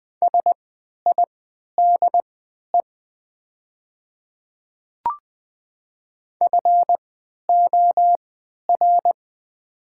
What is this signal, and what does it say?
Morse code, heard as a keyed tone, about 20 wpm, spaced as for 11 wpm, 700 Hz. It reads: SIDE FOR
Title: 3x Character Spacing - Top 200 Words - 20wpm